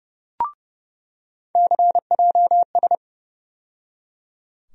Morse code, heard as a keyed tone, about 30 wpm, 700 Hz. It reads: CJS
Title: USA Call Sign Postfixes - 3 Letters 30wpm